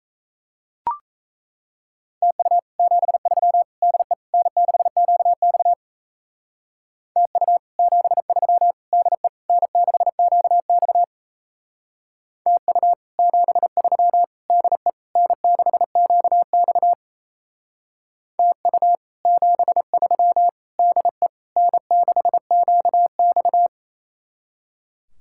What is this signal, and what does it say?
Morse code, heard as a keyed tone, about 36 wpm, 700 Hz. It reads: TU 73 DE N6QX TU 73 DE N6QX TU 73 DE N6QX TU 73 DE N6QX